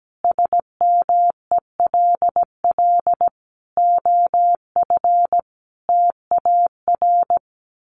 Morse code, heard as a keyed tone, about 17 wpm, 700 Hz. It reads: SMELL OF TAR
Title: A Sound Of Thunder 17wpm